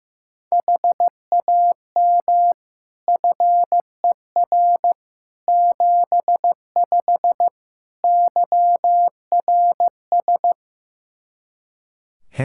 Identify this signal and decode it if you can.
Morse code, heard as a keyed tone, about 16 wpm, 700 Hz. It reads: HAM FER 75 YRS
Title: Practice QSO - Exchange 3 - More Info - 15wpm